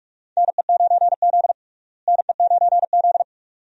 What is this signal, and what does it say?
Morse code, heard as a keyed tone, about 41 wpm, 700 Hz. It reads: NE9Z NE9Z
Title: USA Call Signs - 4 Characters - Repeated 2x - 45wpm